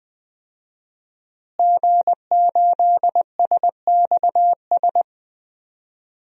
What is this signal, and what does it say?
Morse code, heard as a keyed tone, about 20 wpm, 700 Hz. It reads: G8SXS
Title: DXCC-100 Call Signs - 5 Characters - Repeated 3x - 20wpm